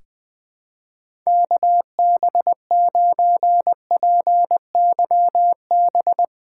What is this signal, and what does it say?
Morse code, heard as a keyed tone, about 20 wpm, 700 Hz. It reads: KB9PYB
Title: USA Callsign DE Callsign 20wpm